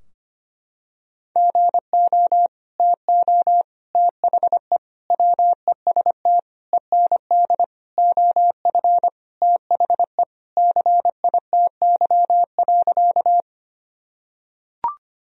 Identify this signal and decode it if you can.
Morse code, heard as a keyed tone, about 25 wpm, 700 Hz. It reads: GO TO THE WEST END OF THE CITY.
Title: Sentences from Top 1000 Words - Review Part 2 - 25wpm